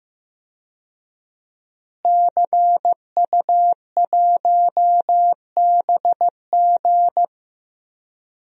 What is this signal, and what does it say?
Morse code, heard as a keyed tone, about 15 wpm, 700 Hz. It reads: CU1BG
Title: DXCC-100 Call Signs - 5 Characters - Repeated 2x - 15wpm